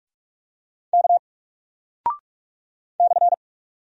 Morse code, heard as a keyed tone, about 44 wpm, 700 Hz. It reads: K /
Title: Single Character (Letter:Number:Punctuation:Prosign) 45wpm